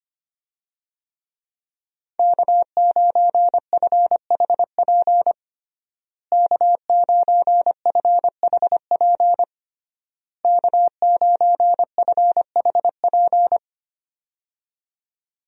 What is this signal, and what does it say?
Morse code, heard as a keyed tone, about 25 wpm, 700 Hz. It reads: K9FHP K9FHP K9FHP
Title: USA Call Signs - 5 Characters - Repeated 3x - 25wpm